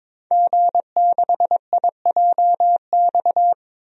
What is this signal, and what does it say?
Morse code, heard as a keyed tone, about 22 wpm, 700 Hz. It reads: G6IJX
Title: DXCC-100 Call Signs - 5 Characters - 22wpm